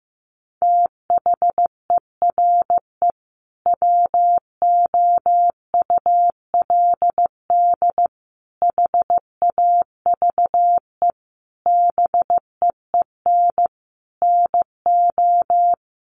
Morse code, heard as a keyed tone, about 15 wpm, 700 Hz. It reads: THERE WOULD HAVE BEEN NO
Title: Sentences from Top 1400 Words 15wpm